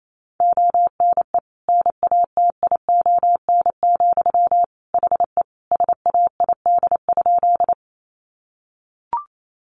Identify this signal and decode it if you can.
Morse code, heard as a keyed tone, about 28 wpm, 700 Hz. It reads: ONE NATION, HE SAID?
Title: Sentences from Top 900 Words 28wpm